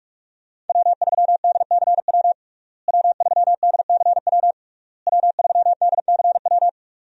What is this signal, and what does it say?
Morse code, heard as a keyed tone, about 46 wpm, 700 Hz. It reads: W3DCW W3DCW W3DCW
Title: USA Call Signs - 5 Characters - Repeated 3x - 45wpm